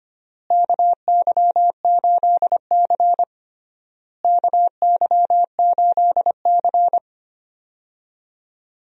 Morse code, heard as a keyed tone, about 25 wpm, 700 Hz. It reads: KY8C KY8C
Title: USA Call Signs - 4 Characters - Repeated 3x - 25wpm